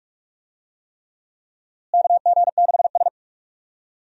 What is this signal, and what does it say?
Morse code, heard as a keyed tone, about 42 wpm, 700 Hz. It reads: KG6S
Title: USA Call Signs - 4 Characters - Repeated 3x - 45wpm